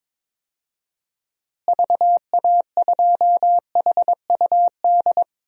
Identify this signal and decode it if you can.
Morse code, heard as a keyed tone, about 22 wpm, 700 Hz. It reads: VA2HUD
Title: DXCC-100 Call Signs - 6 Characters - Repeated 2x - 22wpm